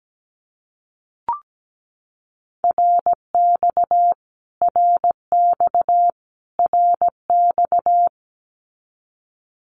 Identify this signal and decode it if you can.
Morse code, heard as a keyed tone, about 17 wpm, 700 Hz. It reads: RX RX RX
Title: Abbreviations Repeated 3x - 17wpm